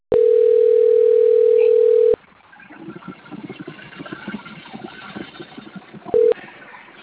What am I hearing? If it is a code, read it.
Ringback tone (North America), 440 + 480 Hz.
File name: out- - - - - .26500